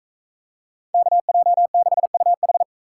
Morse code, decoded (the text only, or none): KJ6UH